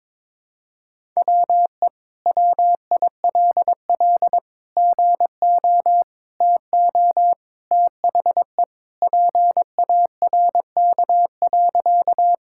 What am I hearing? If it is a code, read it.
Morse code, heard as a keyed tone, about 22 wpm, 700 Hz. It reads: WE WILL GO TO THE PARK.
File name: practice / 22wpm / Sentences with 2-Word Phrases - 22wpm